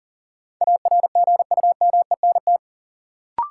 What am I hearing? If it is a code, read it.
Morse code, heard as a keyed tone, about 40 wpm, 700 Hz. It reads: ARGUMENT
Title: Sentences from Top 1400 Words - New Word Drill - 40wpm